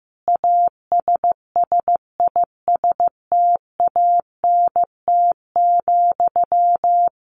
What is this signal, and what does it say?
Morse code, heard as a keyed tone, about 15 wpm, 700 Hz. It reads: ASSISTANT,
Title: A Sound Of Thunder 15wpm